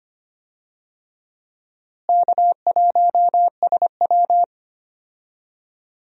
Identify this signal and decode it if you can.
Morse code, heard as a keyed tone, about 25 wpm, 700 Hz. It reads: K1SW
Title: USA Call Signs - 4 Characters - Repeated 3x - 25wpm